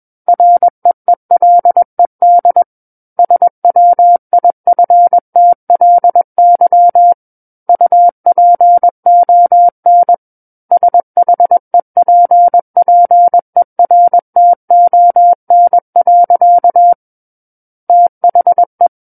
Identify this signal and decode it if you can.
Morse code, heard as a keyed tone, about 21 wpm, 700 Hz. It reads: REELED SWIFTLY UPON SHEPPERTON. THE
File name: Book One / 12-Chapter 12 - 21 WPM